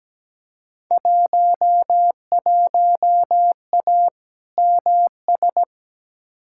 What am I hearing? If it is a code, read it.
Morse code, heard as a keyed tone, about 17 wpm, 700 Hz. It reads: 11A MS
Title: Field Day Exchanges Encore - 17wpm